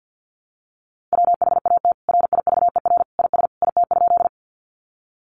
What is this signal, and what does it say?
Morse code, heard as a keyed tone, about 50 wpm, 700 Hz. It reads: WHAT RIVER IS IT?
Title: Sentences from Top 300 Words 50wpm